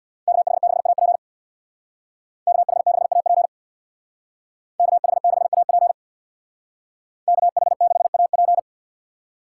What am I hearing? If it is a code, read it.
Morse code, heard as a keyed tone, about 64 wpm, 700 Hz. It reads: KH6AP KH6AP KH6AP KH6AP